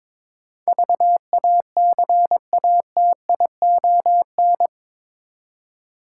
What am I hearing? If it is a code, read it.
Morse code, heard as a keyed tone, about 21 wpm, 700 Hz. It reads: VACATION
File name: practice / 22wpm / Eight Letter Words - 22wpm